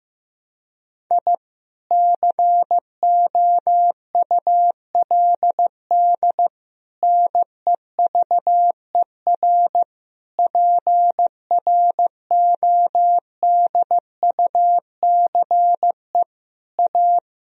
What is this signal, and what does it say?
Morse code, heard as a keyed tone, about 16 wpm, 700 Hz. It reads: I COULD NEVER PRODUCE A
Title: Sentences from Top 1000 Words - Review Part 3 - 15wpm